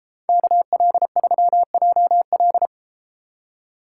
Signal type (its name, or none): Morse code